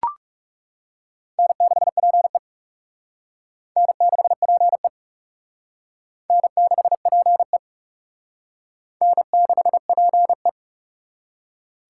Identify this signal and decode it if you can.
Morse code, heard as a keyed tone, about 37 wpm, 700 Hz. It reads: N6PE N6PE N6PE N6PE
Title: Speed Racing - USA Call Signs 30wpm